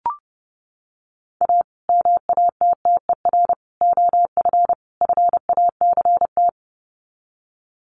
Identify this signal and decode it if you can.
Morse code, heard as a keyed tone, about 30 wpm, 700 Hz. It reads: A MATTER OF FACT
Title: Sets of 4 Words 30wpm